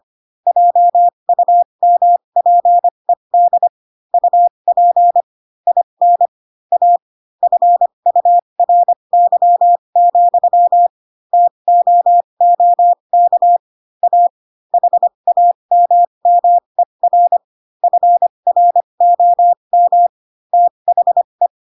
Morse code, heard as a keyed tone, about 25 wpm, spaced as for 22 wpm, 700 Hz. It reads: JUMPED UP IN A FURY, TOOK A HAMMER FROM THE
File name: Pinocchio / Pinocchio ch4 18 25 ik1hge